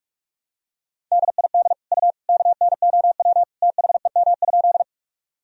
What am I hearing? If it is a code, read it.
Morse code, heard as a keyed tone, about 46 wpm, 700 Hz. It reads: DID U KNOW THEM?
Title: Sentences from Top 100 Words and Abbreviations 45wpm